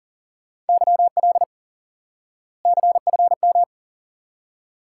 Morse code, heard as a keyed tone, about 40 wpm, 700 Hz. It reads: YL CFM
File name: IQE Course 3 - Lesson 89 - YAGI - Two Elements - 40wpm